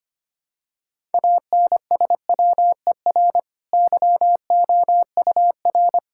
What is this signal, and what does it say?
Morse code, heard as a keyed tone, about 25 wpm, 690 Hz. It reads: ANSWER YOUR
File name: Easy Sentences with Hints - 25wpm